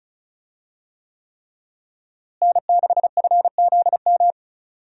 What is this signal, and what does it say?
Morse code, heard as a keyed tone, about 34 wpm, 700 Hz. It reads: N6FZM
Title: USA Call Signs - 5 Characters - Repeated 3x - 35wpm